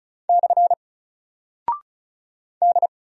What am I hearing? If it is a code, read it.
Morse code, heard as a keyed tone, about 36 wpm, 700 Hz. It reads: / D
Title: Single Character (Letter:Number:Punctuation:Prosign) 35wpm